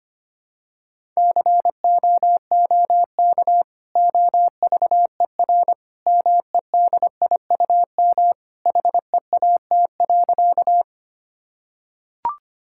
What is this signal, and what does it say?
Morse code, heard as a keyed tone, about 25 wpm, 700 Hz. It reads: COOK OVER MEDIUM HEAT.
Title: Sentences from Top 1000 Words - Review Part 6 - 25wpm